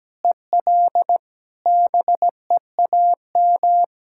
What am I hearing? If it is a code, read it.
Morse code, heard as a keyed tone, about 17 wpm, 700 Hz. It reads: EL BEAM
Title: Practice QSO - Exchange 2 - Info - 17wpm